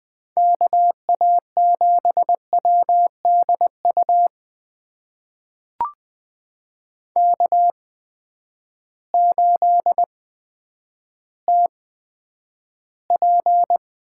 Morse code, heard as a keyed tone, about 20 wpm, spaced as for 10 wpm, 700 Hz. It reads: KA7WDU K8TP
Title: USA Call Signs - 8x Character Spacing - 20wpm